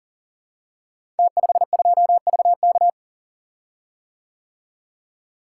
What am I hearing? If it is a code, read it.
Morse code, heard as a keyed tone, about 42 wpm, 700 Hz. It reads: T52VK
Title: Code Groups - 5 Letter:Numbers - 40wpm